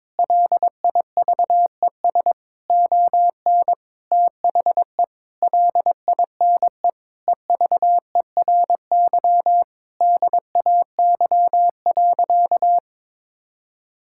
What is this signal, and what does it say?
Morse code, heard as a keyed tone, about 22 wpm, 700 Hz. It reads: LIVES ON THE LINE EVERY DAY.
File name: practice / 22wpm / Sentences with 8-Word Phrases - 22wpm